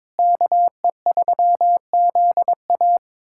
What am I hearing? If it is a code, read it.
Morse code, heard as a keyed tone, about 22 wpm, 700 Hz. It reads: KE3ZA
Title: USA Callsign DE Callsign - 5 Characters - 22wpm